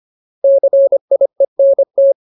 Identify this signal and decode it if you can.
Morse code, heard as a keyed tone, about 25 wpm, 550 Hz. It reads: CIENT
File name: Ultimate Word List / 25WPM Ultimate